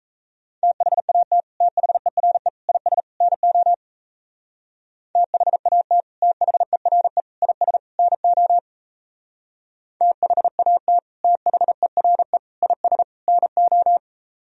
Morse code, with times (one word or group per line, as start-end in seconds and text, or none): 0.6-1.4 THAT
1.6-2.5 THERE
2.7-3.0 IS
3.2-3.7 NO
5.2-6.0 THAT
6.2-7.2 THERE
7.4-7.8 IS
8.0-8.6 NO
10.0-11.0 THAT
11.2-12.4 THERE
12.6-13.0 IS
13.3-14.0 NO